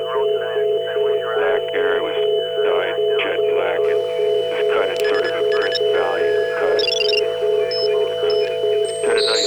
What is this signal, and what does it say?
Congestion tone (425 Hz), with speech.